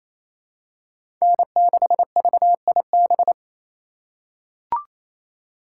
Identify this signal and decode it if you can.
Morse code, heard as a keyed tone, about 28 wpm, 700 Hz. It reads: N6VIB
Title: USA Call Signs - 5 Characters - 2x Character Spacing - 28wpm